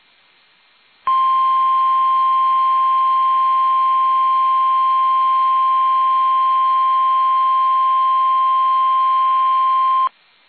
Emergency Alert System (SAME) tones (the 1050 Hz attention tone). It is heard from 1.0 to 10.1 s.
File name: eas / KUSO / KUSO